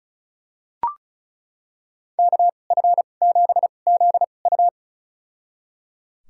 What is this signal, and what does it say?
Morse code, heard as a keyed tone, about 36 wpm, spaced as for 25 wpm, 700 Hz. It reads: KF7ZU
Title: USA Call Signs - 5 Characters - 2x Character Spacing - 35wpm